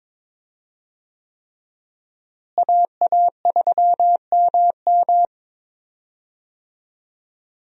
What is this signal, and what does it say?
Morse code, heard as a keyed tone, about 22 wpm, 700 Hz. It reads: AA3MM
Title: USA Call Signs - 5 Characters - Repeated 3x - 22wpm